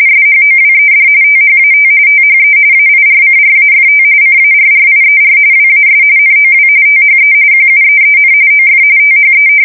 RTTY, about 45.45 baud. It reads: SHOWED THAT THE AIRCRAFT CARRIER WAS NOW THE QUEEN OF THE
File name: fsk170 / battleshipnj